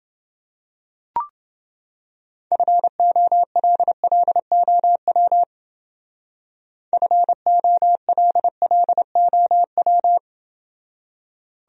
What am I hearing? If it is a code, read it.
Morse code, heard as a keyed tone, about 29 wpm, 700 Hz. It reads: FOLLOW FOLLOW